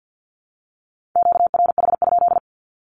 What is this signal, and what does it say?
Morse code, heard as a keyed tone, about 50 wpm, 700 Hz. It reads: QRH?
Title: Q-Codes Repeated 3x 50wpm